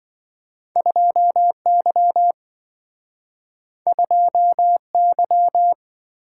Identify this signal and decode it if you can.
Morse code, heard as a keyed tone, about 27 wpm, 700 Hz. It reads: 2Y 2Y